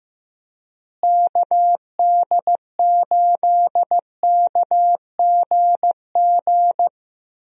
Morse code, heard as a keyed tone, about 15 wpm, 700 Hz. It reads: KD8KGG
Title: USA Call Signs - 6 Characters - Repeated 3x - 15wpm